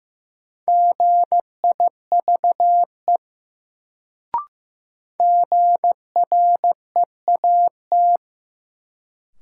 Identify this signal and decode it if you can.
Morse code, heard as a keyed tone, about 15 wpm, 700 Hz. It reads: GIVE GREAT
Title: Sets of 2 Words - New Word Drill - 15wpm